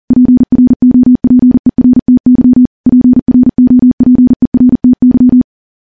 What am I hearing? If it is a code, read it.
Morse code, heard as a keyed tone, about 40 wpm, 250 Hz. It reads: PROPERTY PROPERTY
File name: morse-training-40WPM